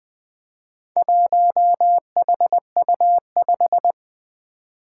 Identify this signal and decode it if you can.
Morse code, heard as a keyed tone, about 20 wpm, 700 Hz. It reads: 1HU5
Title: Four Challenging Letter-Number Combinations 20wpm